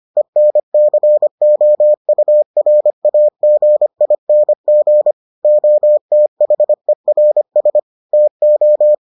Morse code, heard as a keyed tone, about 25 wpm, 600 Hz. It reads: ENCOURAGING OTHERS TO